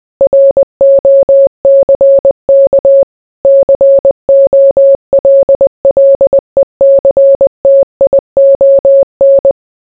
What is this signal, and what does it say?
Morse code, heard as a keyed tone, about 20 wpm, 550 Hz. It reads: ROCK COLLECTION